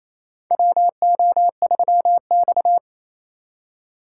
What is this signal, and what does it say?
Morse code, heard as a keyed tone, about 28 wpm, 700 Hz. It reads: WO3X
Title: USA Call Signs - 4 Characters - 4x Character Spacing - 28wpm